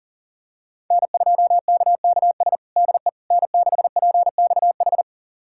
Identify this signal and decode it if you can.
Morse code, heard as a keyed tone, about 40 wpm, 700 Hz. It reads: N2KKS DE N6PXH
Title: USA Callsign DE Callsign - 5 Characters - 40wpm